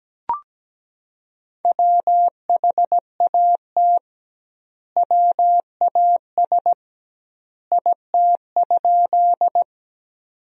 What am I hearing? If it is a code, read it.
Morse code, heard as a keyed tone, about 17 wpm, 700 Hz. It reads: WHAT WAS IT?
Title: 2x Word-Spacing - Sentences from Top 100 Words - 17wpm